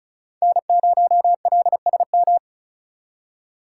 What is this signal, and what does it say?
Morse code, heard as a keyed tone, about 36 wpm, 700 Hz. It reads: N0LSM
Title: USA Call Signs - 5 Characters - 4x Character Spacing - 35wpm